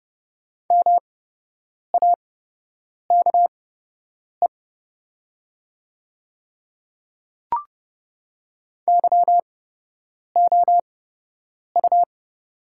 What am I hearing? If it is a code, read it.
Morse code, heard as a keyed tone, about 29 wpm, spaced as for 8 wpm, 700 Hz. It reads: MAKE YOU
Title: 8x Character Spacing - Top 100 Words - 30wpm